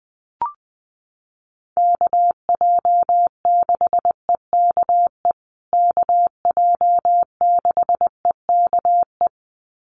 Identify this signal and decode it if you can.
Morse code, heard as a keyed tone, about 20 wpm, 700 Hz. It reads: KJ6EKE KJ6EKE
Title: USA Call Signs Repeated 2x 20wpm